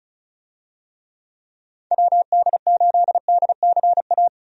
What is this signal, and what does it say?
Morse code, heard as a keyed tone, about 36 wpm, 700 Hz. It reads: WD8DCA
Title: USA Call Signs - 6 Characters - 35wpm